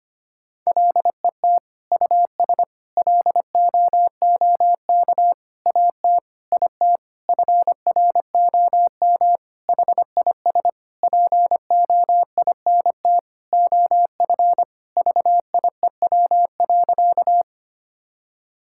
Morse code, heard as a keyed tone, about 25 wpm, 700 Hz. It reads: LET US LOOK AT IT FROM HIS POINT OF VIEW.
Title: Sentences from Top 1000 Words - Review Part 6 - 25wpm